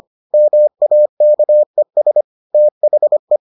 Morse code, heard as a keyed tone, about 25 wpm, 600 Hz. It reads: MAKES THE